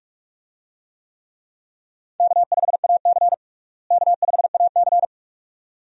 Morse code, heard as a keyed tone, about 46 wpm, 700 Hz. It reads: K5AC K5AC
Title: USA Call Signs - 4 Characters - Repeated 2x - 45wpm